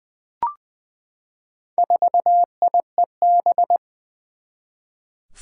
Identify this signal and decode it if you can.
Morse code, heard as a keyed tone, about 20 wpm, 700 Hz. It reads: 4IEB
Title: Four Challenging Letter-Number Combinations 20wpm